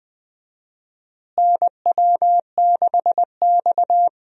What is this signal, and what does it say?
Morse code, heard as a keyed tone, about 20 wpm, 700 Hz. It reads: NW6X
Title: USA Call Signs - 4 Characters - Repeated 2x - 20wpm